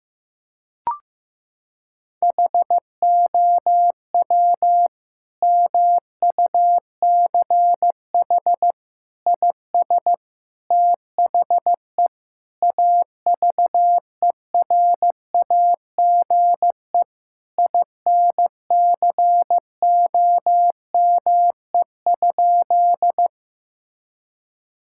Morse code, heard as a keyed tone, about 15 wpm, 700 Hz. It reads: HOW MUCH IS THE AVERAGE INCOME?
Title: Sentences from Top 1000 Words - Review Part 6 - 15wpm